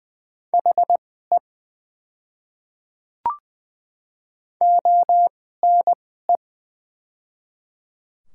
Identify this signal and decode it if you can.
Morse code, heard as a keyed tone, about 20 wpm, spaced as for 14 wpm, 700 Hz. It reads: HE ONE
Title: 2x Character Spacing - Top 100 Words - 20wpm